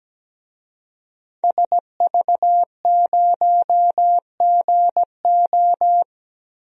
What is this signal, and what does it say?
Morse code, heard as a keyed tone, about 17 wpm, 700 Hz. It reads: SV0GO
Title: DXCC-100 Call Signs - 5 Characters - 17wpm